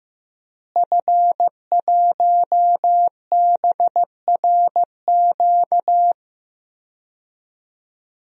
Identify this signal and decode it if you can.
Morse code, heard as a keyed tone, about 15 wpm, 700 Hz. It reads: F1BRQ
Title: DXCC-100 Call Signs - 5 Characters - Repeated 3x - 15wpm